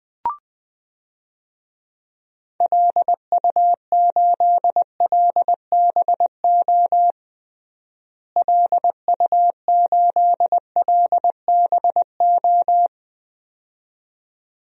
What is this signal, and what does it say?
Morse code, heard as a keyed tone, about 20 wpm, 700 Hz. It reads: LU8LBO LU8LBO
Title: DXCC-100 Call Signs - 6 Characters - Repeated 2x - 20wpm